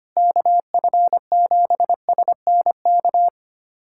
Morse code, heard as a keyed tone, about 25 wpm, 700 Hz. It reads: KF7SNK
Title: USA Callsign DE Callsign 25wpm